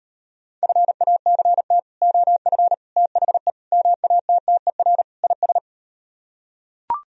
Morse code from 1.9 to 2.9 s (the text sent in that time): OF